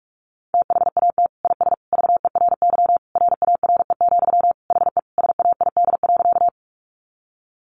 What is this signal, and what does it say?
Morse code, heard as a keyed tone, about 44 wpm, 700 Hz. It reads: THAT IS VERY RARE, HE SAID.
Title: Sentences from Top 1100 Words 45wpm